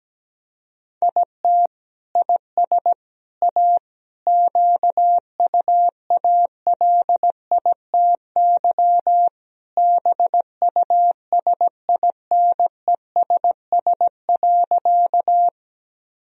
Morse code, heard as a keyed tone, about 17 wpm, 700 Hz. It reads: IT IS A QUALITY BUSINESS.
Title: Sentences from Top 1000 Words - Review Part 4 - 17wpm